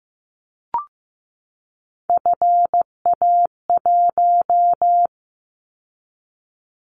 Morse code, heard as a keyed tone, about 15 wpm, 700 Hz. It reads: FA1
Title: Callsign Prefixes for Top 50 DXCC - 15wpm